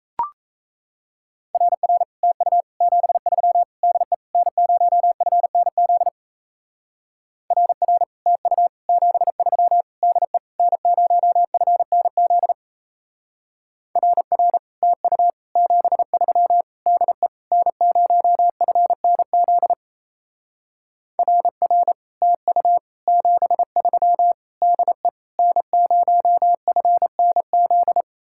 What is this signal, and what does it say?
Morse code, heard as a keyed tone, about 36 wpm, 700 Hz. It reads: RR TU 73 DE N0FNZ RR TU 73 DE N0FNZ RR TU 73 DE N0FNZ RR TU 73 DE N0FNZ